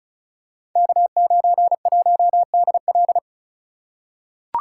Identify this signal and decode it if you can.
Morse code, heard as a keyed tone, about 36 wpm, 700 Hz. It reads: K91DL
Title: Code Groups - 5 Letter:Numbers - 35wpm